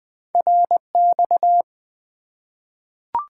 Morse code, heard as a keyed tone, about 20 wpm, 700 Hz. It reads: RX
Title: Top 90 QSO Elements 20wpm